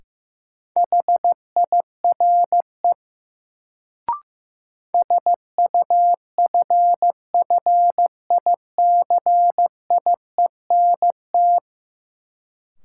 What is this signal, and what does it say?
Morse code, heard as a keyed tone, about 15 wpm, 700 Hz. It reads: HIRE SUFFICIENT
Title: Sentences from Top 1400 Words - New Word Drill Alt - 15wpm